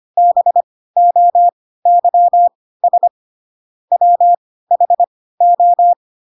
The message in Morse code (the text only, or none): BOYS WHO